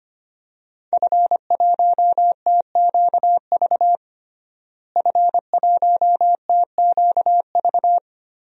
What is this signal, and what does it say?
Morse code, heard as a keyed tone, about 25 wpm, 700 Hz. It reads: F1TQV F1TQV
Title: DXCC-100 Call Signs - 5 Characters - Repeated 2x - 25wpm